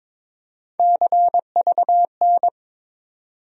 Morse code, heard as a keyed tone, about 22 wpm, 700 Hz. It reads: CVN
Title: USA Call Sign Postfixes - 3 Letters 22wpm